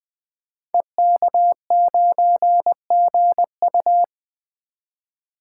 Morse code, heard as a keyed tone, about 20 wpm, 700 Hz. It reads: EK9GU
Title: Code Groups - 5 Letter:Numbers - 20wpm